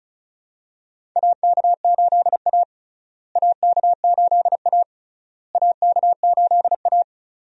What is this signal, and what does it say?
Morse code, heard as a keyed tone, about 36 wpm, 700 Hz. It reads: AK8A AK8A AK8A
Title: USA Call Signs - 4 Characters - Repeated 3x - 35wpm